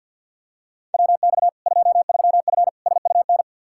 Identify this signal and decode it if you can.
Morse code, heard as a keyed tone, about 51 wpm, 700 Hz. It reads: WX 23F SUN
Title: Practice QSO - Exchange 2 - Info - 50wpm